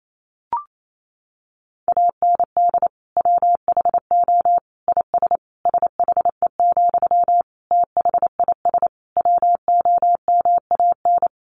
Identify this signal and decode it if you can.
Morse code, heard as a keyed tone, about 28 wpm, 700 Hz. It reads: AND WHO IS SHE, THIS WOMAN
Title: Sentences from Top 1100 Words 28wpm